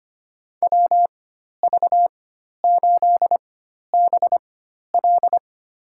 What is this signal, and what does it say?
Morse code, heard as a keyed tone, about 25 wpm, spaced as for 12 wpm, 700 Hz. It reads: WV8BL
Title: USA Call Signs - 4x Character Spacing - 25wpm